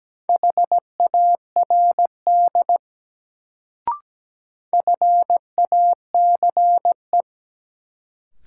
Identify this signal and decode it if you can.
Morse code, heard as a keyed tone, about 17 wpm, 700 Hz. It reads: HARD FACE
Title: Sentences from Top 300 Words - New Word Drill - 17wpm